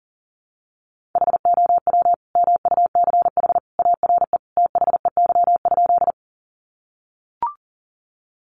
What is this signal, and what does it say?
Morse code, heard as a keyed tone, about 40 wpm, 700 Hz. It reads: HOW MUCH ARE THEY?
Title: Sentences from Top 200 Words 40wpm